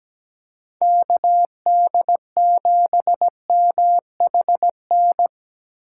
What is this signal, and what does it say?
Morse code, heard as a keyed tone, about 17 wpm, 700 Hz. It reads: KD7MHN
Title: USA Call Signs - 6 Characters - 3x Character Spacing - 17wpm